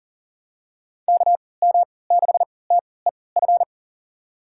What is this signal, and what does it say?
Morse code, heard as a keyed tone, about 38 wpm, spaced as for 22 wpm, 700 Hz. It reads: KM6TEF
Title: USA Call Signs - 6 Characters - 3x Character Spacing - 40wpm